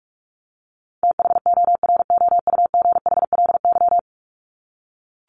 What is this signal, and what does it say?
Morse code, heard as a keyed tone, about 44 wpm, 700 Hz. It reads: THOROUGHLY
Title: LY Suffix - Top 100 Words with Base Words 45wpm